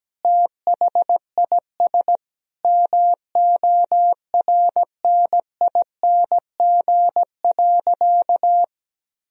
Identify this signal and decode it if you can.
Morse code, heard as a keyed tone, about 17 wpm, 700 Hz. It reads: THIS MORNING.
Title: Sentences from Top 1000 Words - Review Part 4 - 17wpm